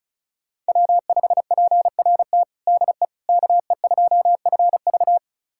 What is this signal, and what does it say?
Morse code, heard as a keyed tone, about 36 wpm, 700 Hz. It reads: W5PRT DE KE2FV
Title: USA Callsign DE Callsign - 5 Characters - 35wpm